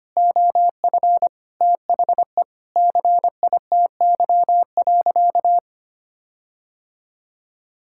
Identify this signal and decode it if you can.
Morse code, heard as a keyed tone, about 25 wpm, 700 Hz. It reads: OF THE CITY.
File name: practice / 25wpm / Sentences with 4-Word Phrases - 25wpm